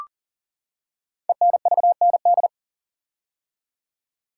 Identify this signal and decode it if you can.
Morse code, heard as a keyed tone, about 41 wpm, 700 Hz. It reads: ENVND
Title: Code Groups - 5 Letters - 40wpm